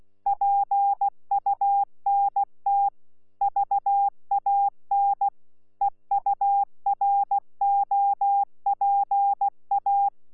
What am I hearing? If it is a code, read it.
Morse code, heard as a keyed tone, about 16 wpm, 800 Hz. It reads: PUNT VAN EUROPA
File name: Z PI4VRZ / PI4VRZ 14-jun-2025